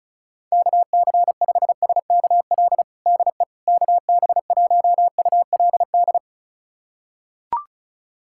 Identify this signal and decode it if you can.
Morse code, heard as a keyed tone, about 36 wpm, 700 Hz. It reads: KC5SKL DE KB1ULD
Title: USA Callsign DE Callsign - 6 Characters - 35wpm